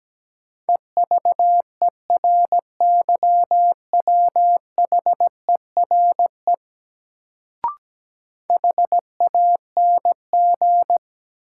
Morse code, heard as a keyed tone, about 17 wpm, 700 Hz. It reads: EVERYWHERE HANG
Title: Sentences from Top 1600 Words - New Word Drill - 17wpm